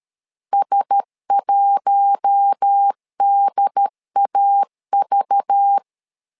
Morse code, heard as a keyed tone, about 13 wpm, 800 Hz. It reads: S1DAV
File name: LEZIONI / 5 LBVC / 5 LBV 12wpm 1855